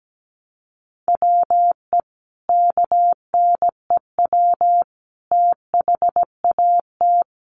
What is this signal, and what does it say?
Morse code, heard as a keyed tone, about 17 wpm, 700 Hz. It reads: WE KNEW THAT
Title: Sentences from Top 400 Words 17wpm